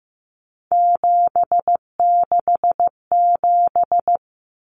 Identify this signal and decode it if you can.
Morse code, heard as a keyed tone, about 15 wpm, 700 Hz. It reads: 767
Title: Numbers - 3-Digit 15wpm